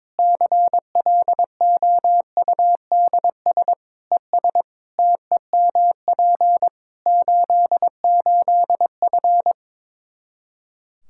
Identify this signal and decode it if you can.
Morse code, heard as a keyed tone, about 22 wpm, 700 Hz. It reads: CLOUDS ES TEMP 88F
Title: K1USN SST QSO Practice 22wpm